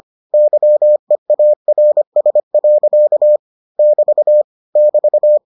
Morse code, heard as a keyed tone, about 25 wpm, 600 Hz. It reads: YEARS. <BT> <BT>